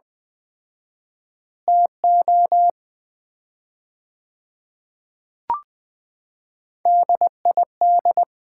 Morse code, heard as a keyed tone, about 20 wpm, 700 Hz. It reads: TO DID
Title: 4x Word-Spacing - Sets of 4 Words - 20wpm